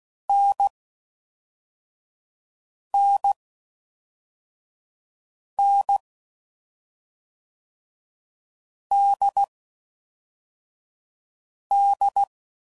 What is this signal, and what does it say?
Morse code, heard as a keyed tone, about 16 wpm, 800 Hz. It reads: N N N D D
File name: Combinations / Difficult Combos / NDB NDB6